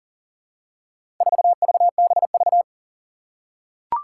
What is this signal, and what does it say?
Morse code, heard as a keyed tone, about 40 wpm, 700 Hz. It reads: 4VBV
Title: Four Challenging Letter-Number Combinations 40wpm